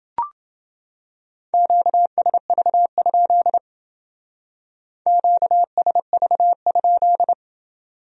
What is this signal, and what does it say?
Morse code, heard as a keyed tone, about 28 wpm, 700 Hz. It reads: QSV? QSV?